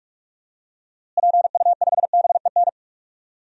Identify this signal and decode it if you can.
Morse code, heard as a keyed tone, about 41 wpm, 700 Hz. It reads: PU5BEN
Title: DXCC-100 Call Signs - 6 Characters - Repeated 3x - 45wpm